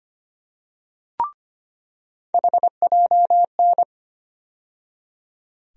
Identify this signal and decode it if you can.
Morse code, heard as a keyed tone, about 25 wpm, 700 Hz. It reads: HJN
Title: USA Call Sign Postfixes - 3 Letters 25wpm